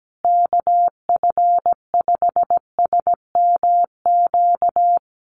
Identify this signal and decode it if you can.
Morse code, heard as a keyed tone, about 17 wpm, 700 Hz. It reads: KF5SMQ
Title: USA Call Signs Repeated 3x 17wpm